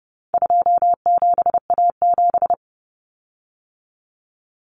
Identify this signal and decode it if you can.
Morse code, heard as a keyed tone, about 30 wpm, 700 Hz. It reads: 27A7
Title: Common Cut Numbers - 4-Digit 30wpm